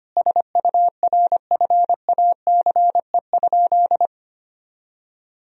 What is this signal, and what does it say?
Morse code, heard as a keyed tone, about 25 wpm, 700 Hz. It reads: SURFACE?
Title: Sentences from Top 1000 Words - Review Part 3 - 25wpm